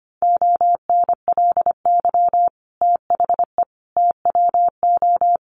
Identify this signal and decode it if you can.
Morse code, heard as a keyed tone, about 25 wpm, 700 Hz. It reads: ONLY THE TWO
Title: Sentences from Top 800 Words 25wpm